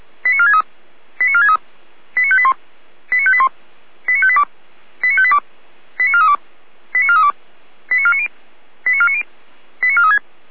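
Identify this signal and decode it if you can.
Five-tone selective call (ZVEI), heard as quick runs of five tones, tones of about 80 ms. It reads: 89463 89463 89581 89581 89582 89582 89432 89432 89490 89490 89436
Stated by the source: ZVEI five-tone call received by SDR, from the feuerwehr-orlamuende.org fire brigade audio archive.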